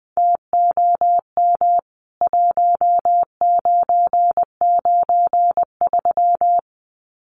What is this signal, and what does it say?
Morse code, heard as a keyed tone, about 20 wpm, 700 Hz. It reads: TOM 1993
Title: CWT Exchange 20wpm